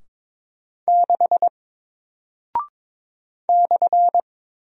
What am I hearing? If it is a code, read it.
Morse code, heard as a keyed tone, about 22 wpm, 700 Hz. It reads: 6 /
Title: Single Character (Letter:Number:Punctuation:Prosign) Alt 22wpm